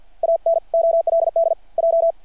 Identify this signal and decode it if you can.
Morse code, heard as a keyed tone, about 52 wpm, 660 Hz. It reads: AN OLD J
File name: archived / CRSnet